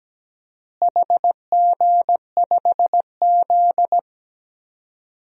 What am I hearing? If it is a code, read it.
Morse code, heard as a keyed tone, about 17 wpm, 700 Hz. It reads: HG5Z
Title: DXCC-100 Call Signs - 4 Characters - Repeated 2x - 17wpm